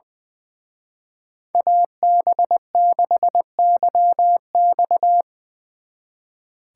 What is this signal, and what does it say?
Morse code, heard as a keyed tone, about 22 wpm, 700 Hz. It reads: AB6YX
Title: Speed Racing - USA 2x2 Call Signs 20wpm